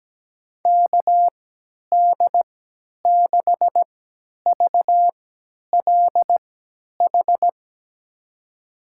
Morse code, heard as a keyed tone, about 17 wpm, spaced as for 10 wpm, 700 Hz. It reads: KD6VLH
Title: USA Call Signs - 6 Characters - 3x Character Spacing - 17wpm